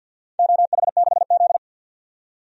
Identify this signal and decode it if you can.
Morse code, heard as a keyed tone, about 52 wpm, 700 Hz. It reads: OH6Z
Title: DXCC-100 Call Signs - 4 Characters - 50wpm